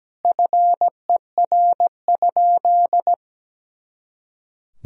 Morse code, heard as a keyed tone, about 17 wpm, 700 Hz. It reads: FER?
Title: Sentences from Top 100 Words and Abbreviations 17wpm